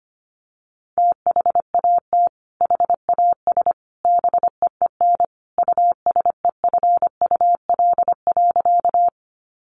Morse code, heard as a keyed tone, about 25 wpm, 700 Hz. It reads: THAT HAS BEEN USEFUL.
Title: Sentences from Top 700 Words 25wpm